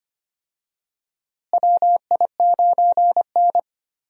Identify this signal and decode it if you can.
Morse code, heard as a keyed tone, about 25 wpm, 700 Hz. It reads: WI9N
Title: USA Call Signs - 4 Characters - Repeated 3x - 25wpm